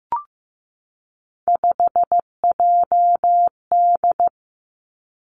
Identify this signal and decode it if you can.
Morse code, heard as a keyed tone, about 15 wpm, 700 Hz. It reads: 5JD
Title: Three Challenging Letter-Number Combinations 15wpm